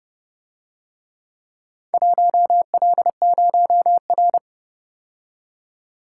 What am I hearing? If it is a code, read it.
Morse code, heard as a keyed tone, about 29 wpm, 700 Hz. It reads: 1L0R